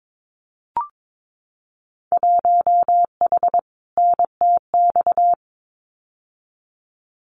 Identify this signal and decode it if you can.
Morse code, heard as a keyed tone, about 22 wpm, 700 Hz. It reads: 1H NTX
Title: Winter Field Day Exchanges 22wpm